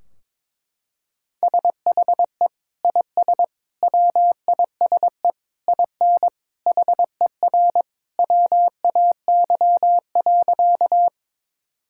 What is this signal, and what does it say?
Morse code, heard as a keyed tone, about 21 wpm, 700 Hz. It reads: SHE IS WISE IN HER WAY.
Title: Sentences from Top 1600 Words 22wpm